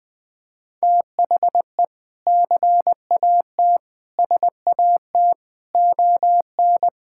Morse code, heard as a keyed tone, about 21 wpm, 700 Hz. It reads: THE CAT SAT ON
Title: Sentences with 3-Word Phrases - 20wpm